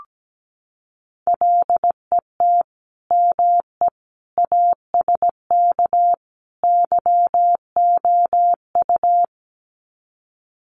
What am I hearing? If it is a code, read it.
Morse code, heard as a keyed tone, about 17 wpm, 700 Hz. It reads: LET ME ASK YOU
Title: Sets of 4 Words 17wpm